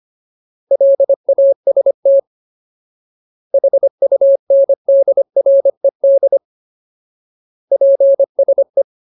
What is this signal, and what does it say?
Morse code, heard as a keyed tone, about 25 wpm, 550 Hz. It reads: LAST HUNDRED PSE
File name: Ultimate Word List / 25WPM Ultimate